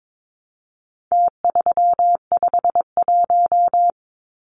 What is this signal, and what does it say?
Morse code, heard as a keyed tone, about 22 wpm, 700 Hz. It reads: T351
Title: Numbers - Contest Style - 4-Digit 22wpm